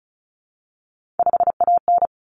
Morse code, heard as a keyed tone, about 34 wpm, 700 Hz. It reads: 5AN
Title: Common Cut Numbers - 3-Digit 35wpm